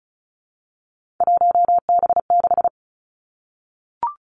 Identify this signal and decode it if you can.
Morse code, heard as a keyed tone, about 36 wpm, 700 Hz. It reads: 1B6
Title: Three Challenging Letter-Number Combinations 35wpm